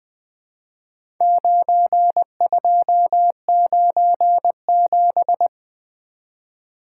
Morse code, heard as a keyed tone, about 20 wpm, 700 Hz. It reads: 9297
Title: Four Letter-Number Combinations - 20wpm